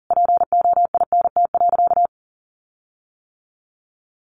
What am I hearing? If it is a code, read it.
Morse code, heard as a keyed tone, about 40 wpm, 700 Hz. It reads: POINT.
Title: Sentences from Top 1300 Words 40wpm